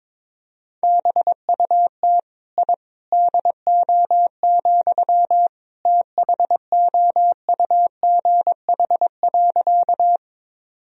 Morse code, heard as a keyed tone, about 22 wpm, 700 Hz. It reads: BUT I DO, THOUGH.
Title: Sentences from Top 1000 Words - Review Part 2 - 22wpm